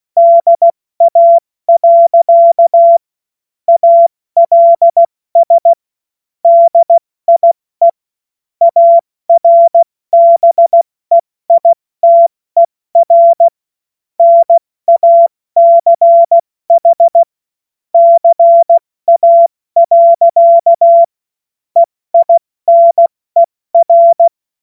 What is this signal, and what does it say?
Morse code, heard as a keyed tone, about 16 wpm, 680 Hz. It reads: DA. ALS DIE ARBEITER NACH CA. EINER
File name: Morsekurs / Witze / Witz 01. August 2023 70 F80